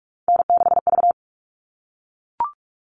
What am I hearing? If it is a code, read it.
Morse code, heard as a keyed tone, about 44 wpm, 700 Hz. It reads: N6V
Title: All Cut Numbers - 3-Digit 45wpm